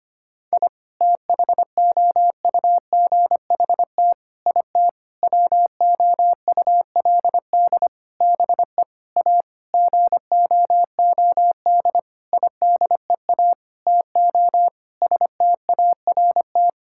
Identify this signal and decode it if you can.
Morse code, heard as a keyed tone, about 25 wpm, 700 Hz. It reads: I THOUGHT IT WOULD BE A GOOD IDEA TO START
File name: practice / 25wpm / Sentences with 9-Word Phrases - 25wpm